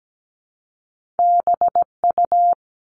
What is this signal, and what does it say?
Morse code, heard as a keyed tone, about 17 wpm, 700 Hz. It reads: BU
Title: Top Two Letter Combinations in Words 17wpm